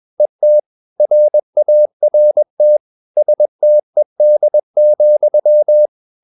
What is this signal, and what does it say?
Morse code, heard as a keyed tone, about 21 wpm, 600 Hz. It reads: ET RART STED,
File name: Ferie QRS 21wpm